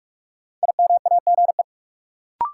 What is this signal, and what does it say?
Morse code, heard as a keyed tone, about 43 wpm, 700 Hz. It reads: IMAGE
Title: Sentences from Top 700 Words - New Word Drill - 45wpm